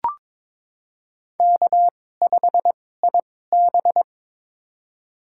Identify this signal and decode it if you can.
Morse code, heard as a keyed tone, about 21 wpm, spaced as for 16 wpm, 700 Hz. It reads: K5IB
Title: USA Call Signs - 4 Characters - 2x Character Spacing - 22wpm